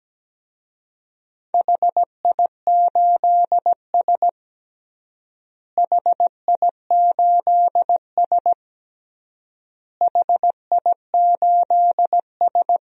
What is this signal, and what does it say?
Morse code, heard as a keyed tone, about 17 wpm, 700 Hz. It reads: HI8S HI8S HI8S
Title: DXCC-100 Call Signs - 4 Characters - Repeated 3x - 17wpm